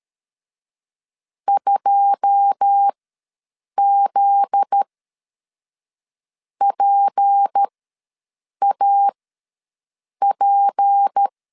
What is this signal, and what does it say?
Morse code, heard as a keyed tone, about 13 wpm, spaced as for 6 wpm, 800 Hz. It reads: 2ZPAP
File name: LEZIONI / 10 SIGNS / 10D COMA ws 12wpm 3618